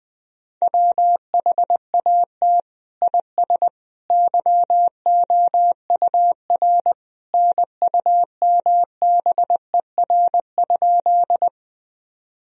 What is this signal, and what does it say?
Morse code, heard as a keyed tone, about 20 wpm, 700 Hz. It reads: WHAT IS YOUR NUMBER?
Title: Sentences from Top 1000 Words - Review Part 1 - 20wpm